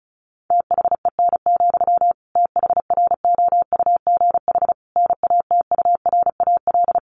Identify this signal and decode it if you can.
Morse code, heard as a keyed tone, about 34 wpm, 700 Hz. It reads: THEN, THROUGH NATURAL
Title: Sentences from Top 1400 Words 35wpm